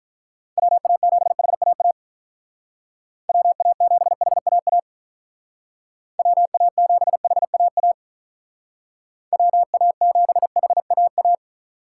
Morse code, heard as a keyed tone, about 44 wpm, 700 Hz. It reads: WA7HAA WA7HAA WA7HAA WA7HAA